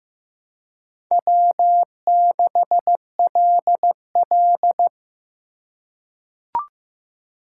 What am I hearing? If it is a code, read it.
Morse code, heard as a keyed tone, about 15 wpm, 700 Hz. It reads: W6LL
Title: USA Call Signs - 4 Characters - 3x Character Spacing - 15wpm